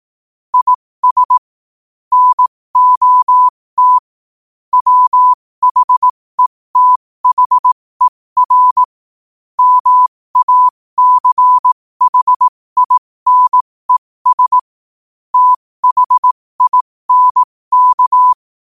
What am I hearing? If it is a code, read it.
Morse code, heard as a keyed tone, about 18 wpm, spaced as for 15 wpm, 1000 Hz. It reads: IS NOT WHETHER MACHINES THINK
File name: qotdfiles / 15wpm